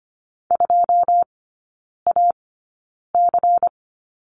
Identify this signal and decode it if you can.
Morse code, heard as a keyed tone, about 25 wpm, spaced as for 9 wpm, 700 Hz. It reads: 2AC
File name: course / 25:10wpm / 093 Lesson 16e - Letter c - Three Characters 25wpm